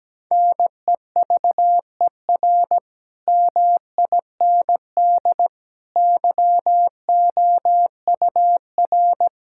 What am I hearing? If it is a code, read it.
Morse code, heard as a keyed tone, about 17 wpm, 700 Hz. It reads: NEVER MIND YOUR
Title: Sentences from Top 1600 Words 17wpm